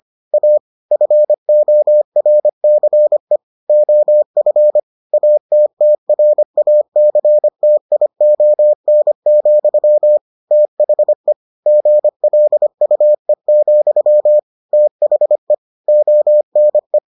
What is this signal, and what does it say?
Morse code, heard as a keyed tone, about 25 wpm, 600 Hz. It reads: A FORCE OF ATTRACTION, THE GLUE, THE ONE